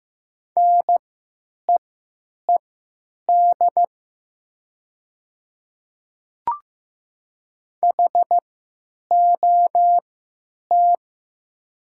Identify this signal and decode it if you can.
Morse code, heard as a keyed tone, about 15 wpm, spaced as for 9 wpm, 700 Hz. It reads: NEED HOT
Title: 3x Character Spacing - Top 200 Words - 15wpm